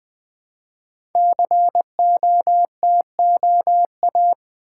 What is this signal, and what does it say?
Morse code, heard as a keyed tone, about 20 wpm, 700 Hz. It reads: COTOA
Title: Code Groups - 5 Letters - 20wpm